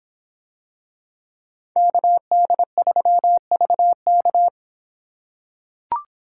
Morse code, heard as a keyed tone, about 25 wpm, 700 Hz. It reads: KD3VK